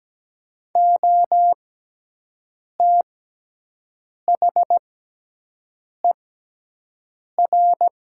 Morse code, heard as a keyed tone, about 17 wpm, spaced as for 6 wpm, 700 Hz. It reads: OTHER
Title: 6x Character Spacing - Top 200 Words - 17wpm